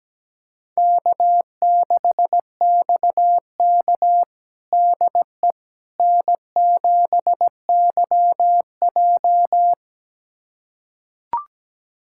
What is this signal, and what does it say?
Morse code, heard as a keyed tone, about 17 wpm, 700 Hz. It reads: K6XK DE N7YJ
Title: USA Callsign DE Callsign - 4 Characters - 17wpm